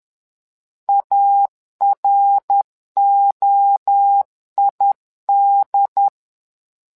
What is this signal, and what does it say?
Morse code, heard as a keyed tone, about 10 wpm, 800 Hz. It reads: AROID
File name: LEZIONI / 3 MWRO / 3 MWRO 10wpm 2148